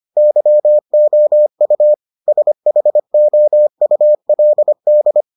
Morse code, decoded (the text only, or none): YOU SHOULD